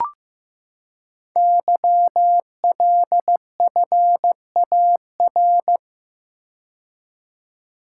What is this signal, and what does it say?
Morse code, heard as a keyed tone, about 15 wpm, 700 Hz. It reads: YLFAR
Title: Code Groups - 5 Letters - 15wpm